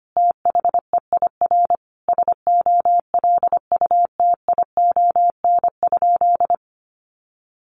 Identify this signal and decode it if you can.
Morse code, heard as a keyed tone, about 25 wpm, 700 Hz. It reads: THEIR SOLUTION?
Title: Sentences from Top 1300 Words 25wpm